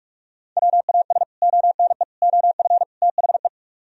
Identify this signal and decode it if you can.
Morse code, heard as a keyed tone, about 58 wpm, 700 Hz. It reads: WAS ONE OF THE